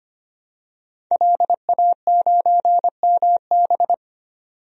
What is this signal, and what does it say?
Morse code, heard as a keyed tone, about 25 wpm, 700 Hz. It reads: LA9MB
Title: DXCC-100 Call Signs - 5 Characters - 25wpm